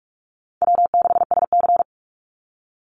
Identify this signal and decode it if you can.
Morse code, heard as a keyed tone, about 44 wpm, 700 Hz. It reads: W6SC
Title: Easy Call Signs Repeated 2x - 45wpm